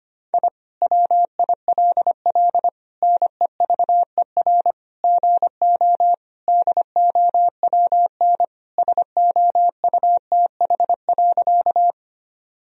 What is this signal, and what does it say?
Morse code, heard as a keyed tone, about 25 wpm, 700 Hz. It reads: I WILL NEVER GO DOWN SOUTH.
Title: Sentences from Top 1000 Words - Review Part 2 - 25wpm